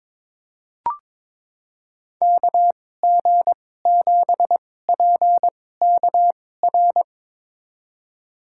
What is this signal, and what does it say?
Morse code, heard as a keyed tone, about 22 wpm, spaced as for 16 wpm, 700 Hz. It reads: KG7PKR
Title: USA Call Signs - 6 Characters - 2x Character Spacing - 22wpm